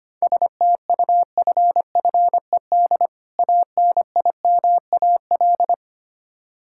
Morse code, heard as a keyed tone, about 25 wpm, 700 Hz. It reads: STUFFED ANIMAL
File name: practice / 25wpm / Difficult Words with Hints - 25wpm